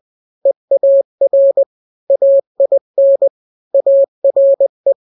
Morse code, heard as a keyed tone, about 20 wpm, 550 Hz. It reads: EAR AIN ARE
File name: n-grams / 3-grams 20 18 WPM